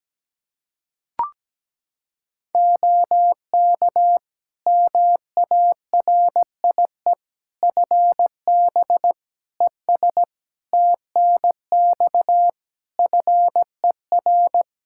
Morse code, heard as a keyed tone, about 17 wpm, 700 Hz. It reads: OK MARIE FB ES TNX FER
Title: Practice QSO - Exchange 2 - Info - 17wpm